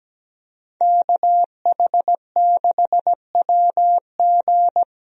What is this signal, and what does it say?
Morse code, heard as a keyed tone, about 17 wpm, 700 Hz. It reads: KH6WG
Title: DXCC-100 Call Signs - 5 Characters - 17wpm